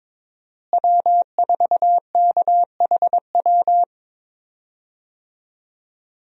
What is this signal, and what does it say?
Morse code, heard as a keyed tone, about 21 wpm, 700 Hz. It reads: W4KHW
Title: USA Call Signs - 5 Characters - Repeated 2x - 22wpm